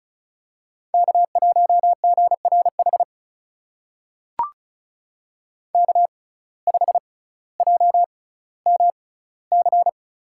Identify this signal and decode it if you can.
Morse code, heard as a keyed tone, about 36 wpm, spaced as for 18 wpm, 700 Hz. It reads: K1GRH K5JMC
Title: USA Call Signs - 6x Character Spacing - 35wpm